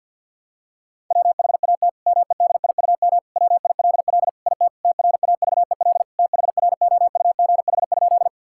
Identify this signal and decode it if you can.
Morse code, heard as a keyed tone, about 45 wpm, 700 Hz. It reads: WHAT MEDIUM WILL IT TRAVEL THROUGH?
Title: Sentences from Top 1000 Words - Review Part 6 - 50wpm